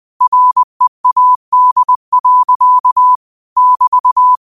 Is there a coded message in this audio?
Yes: Morse code from 0.2 to 4.4 s, heard as a keyed tone, about 20 wpm, 1000 Hz. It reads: READ. <BT>